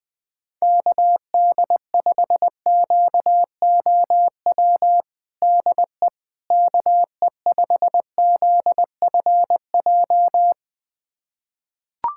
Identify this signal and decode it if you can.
Morse code, heard as a keyed tone, about 21 wpm, 700 Hz. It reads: KD5QOW DE KE5ZFJ
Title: USA Callsign DE Callsign 20wpm